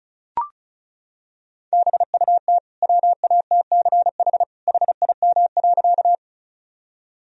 Morse code, heard as a keyed tone, about 36 wpm, 700 Hz. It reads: BUT WATCH HIM.
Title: Sentences from Top 1000 Words - Review Part 1 - 35wpm